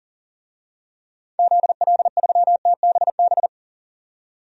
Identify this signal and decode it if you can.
Morse code, heard as a keyed tone, about 38 wpm, 700 Hz. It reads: ZL3TBB
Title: DXCC-100 Call Signs - 6 Characters - 40wpm